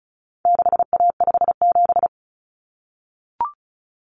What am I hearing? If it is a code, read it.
Morse code, heard as a keyed tone, about 34 wpm, 700 Hz. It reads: 6A57
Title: Common Cut Numbers - 4-Digit 35wpm